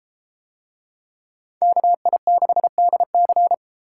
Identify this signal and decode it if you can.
Morse code, heard as a keyed tone, about 32 wpm, 700 Hz. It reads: KI6DC